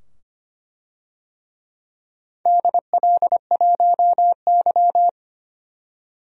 Morse code, heard as a keyed tone, about 25 wpm, 700 Hz. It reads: DL1Y
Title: DXCC-100 Call Signs - 4 Characters - 25wpm